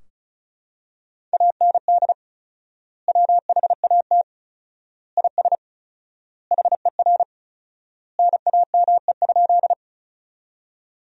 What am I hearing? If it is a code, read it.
Morse code, heard as a keyed tone, about 35 wpm, 700 Hz. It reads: AND WHAT IS HER NAME?
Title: 4x Word-Spacing - Sentences from Top 200 Words - 35wpm